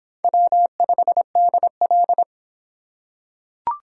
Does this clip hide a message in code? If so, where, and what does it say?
Yes: Morse code from 0.2 to 2.2 s, heard as a keyed tone, about 25 wpm, 700 Hz. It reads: W5DL